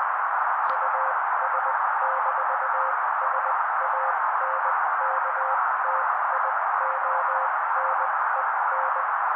Morse code, heard as a keyed tone, about 20 wpm, spaced as for 16 wpm, 530 Hz. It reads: US-SANKTIONEN
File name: daily / 16wpm noise